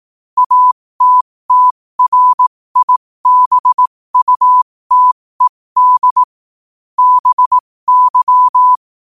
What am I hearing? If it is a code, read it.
Morse code, heard as a keyed tone, about 18 wpm, spaced as for 15 wpm, 1000 Hz. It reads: ATTRIBUTED BY
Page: New quotes every day in morse code at 15 Words per minute.